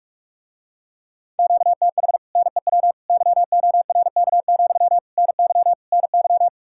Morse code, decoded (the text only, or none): QTH NEW YORK, NY NY